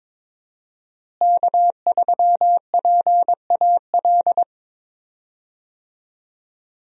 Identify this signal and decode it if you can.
Morse code, heard as a keyed tone, about 22 wpm, 700 Hz. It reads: K3PAL
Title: USA Call Signs - 5 Characters - Repeated 2x - 22wpm